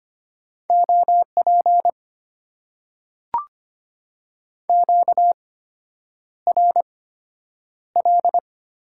Morse code, heard as a keyed tone, about 25 wpm, spaced as for 9 wpm, 700 Hz. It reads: OP QRL
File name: Top 90 QSO Elements - 8x Character Spacing - 25wpm